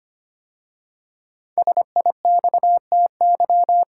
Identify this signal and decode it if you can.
Morse code, heard as a keyed tone, about 25 wpm, 700 Hz. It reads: SIXTY
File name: 5 Letter Words - First 3 Letters Given - 25wpm